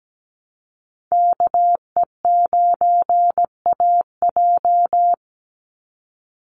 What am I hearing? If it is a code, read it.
Morse code, heard as a keyed tone, about 17 wpm, 700 Hz. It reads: KE9AJ
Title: NA SOTA Chaser Callsigns 17wpm